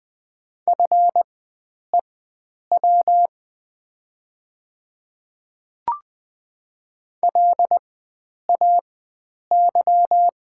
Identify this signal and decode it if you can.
Morse code, heard as a keyed tone, about 20 wpm, spaced as for 9 wpm, 700 Hz. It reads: FEW LAY
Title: 4x Character Spacing - Top 500 Words - 20wpm